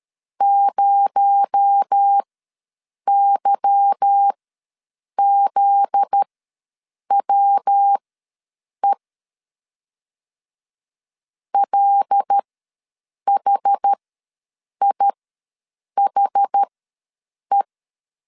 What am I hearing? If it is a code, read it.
Morse code, heard as a keyed tone, about 13 wpm, spaced as for 7 wpm, 800 Hz. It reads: 0YZWE LHIHE
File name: LEZIONI / 9 2867 ws 12wpm 3431